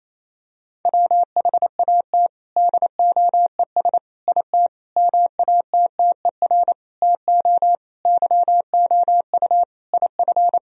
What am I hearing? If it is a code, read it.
Morse code, heard as a keyed tone, about 28 wpm, 700 Hz. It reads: WHAT DOES IT MATTER TO YOU IF
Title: Sentences from Top 1000 Words - Review Part 3 - 28wpm